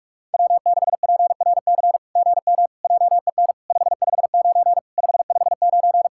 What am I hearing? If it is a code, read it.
Morse code, heard as a keyed tone, about 44 wpm, 700 Hz. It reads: W6PRC GM JEN 559 559
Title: SOTA QSO Practice 45wpm